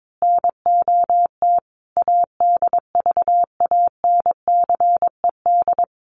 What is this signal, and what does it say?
Morse code, heard as a keyed tone, about 22 wpm, 700 Hz. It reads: NOT ADVANCED
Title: Sentences from Top 1200 Words 22wpm